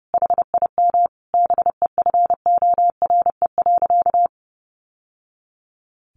Morse code, heard as a keyed tone, about 30 wpm, 700 Hz. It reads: HIM BEFORE.
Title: Sentences from Top 1100 Words 30wpm